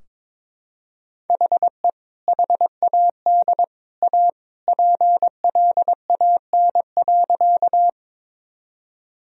Morse code, heard as a keyed tone, about 22 wpm, 700 Hz. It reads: HE HAD A PLAN.
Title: Sentences from Top 1000 Words - Review Part 2 - 22wpm